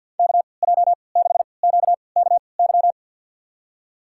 Morse code, heard as a keyed tone, about 51 wpm, spaced as for 28 wpm, 700 Hz. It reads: KJ6QKY
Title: USA Call Signs - 6 Characters - 3x Character Spacing - 50wpm